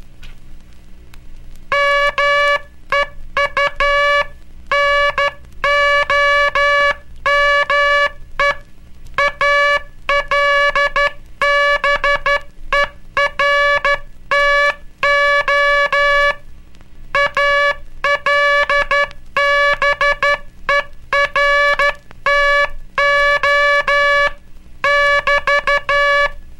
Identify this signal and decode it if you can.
Morse code, heard as a keyed tone, about 12 wpm, 1700 Hz. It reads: MEU NOME ALBERTO ALBERTO <BT>
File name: cursos / Curso CW mp3 / CW Aula - 18